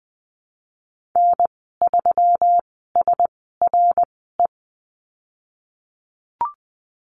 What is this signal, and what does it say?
Morse code, heard as a keyed tone, about 20 wpm, spaced as for 15 wpm, 700 Hz. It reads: N3SRE
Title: 122 Lesson 20g - Number 3 - Call Signs 20wpm